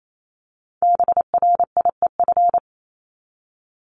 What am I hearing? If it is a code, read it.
Morse code, heard as a keyed tone, about 28 wpm, 700 Hz. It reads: BRIEF
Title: LY Suffix - Top 100 Words with Base Words 28wpm